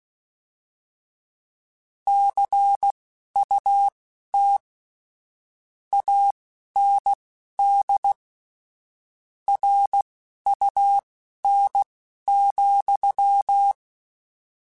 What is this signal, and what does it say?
Morse code, heard as a keyed tone, about 16 wpm, spaced as for 11 wpm, 800 Hz. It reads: CUT AND RUN,
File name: Misc / Misc-mp3s / Word Pairs @10